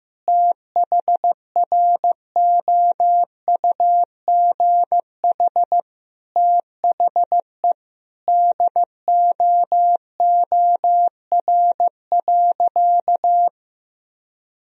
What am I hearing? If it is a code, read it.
Morse code, heard as a keyed tone, about 16 wpm, 700 Hz. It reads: THROUGH THE DOOR.
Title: Sentences from Top 1000 Words - Review Part 1 - 15wpm